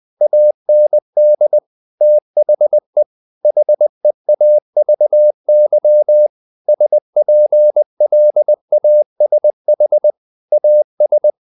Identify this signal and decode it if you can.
Morse code, heard as a keyed tone, about 20 wpm, 600 Hz. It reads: AND THE HEAVY SPLASH AS
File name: the old man and the sea part 2 20wpm